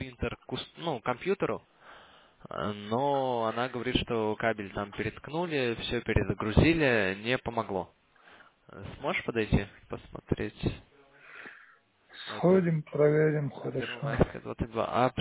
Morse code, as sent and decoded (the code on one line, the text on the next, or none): none